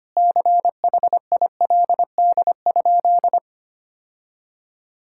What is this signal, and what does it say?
Morse code, heard as a keyed tone, about 25 wpm, 700 Hz. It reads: CHILD?
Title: Sentences from Top 1600 Words 25wpm